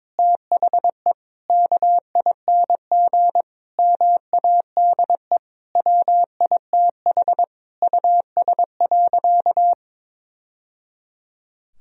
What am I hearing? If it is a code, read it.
Morse code, heard as a keyed tone, about 22 wpm, 700 Hz. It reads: THE KING MADE WITH US.
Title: Sentences from Top 1000 Words - Review Part 4 - 22wpm